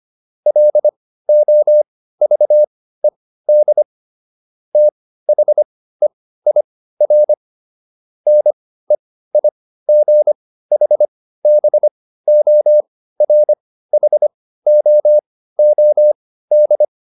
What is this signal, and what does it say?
Morse code, heard as a keyed tone, about 25 wpm, spaced as for 15 wpm, 600 Hz. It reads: LOVED THEIR NEIGHBORHOOD